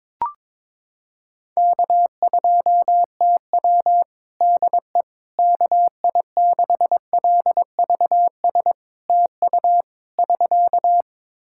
Morse code, heard as a keyed tone, about 22 wpm, 700 Hz. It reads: K2TW DE KI6LVS TU <SK>
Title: Practice QSO - Exchange 4 - Ending - 22wpm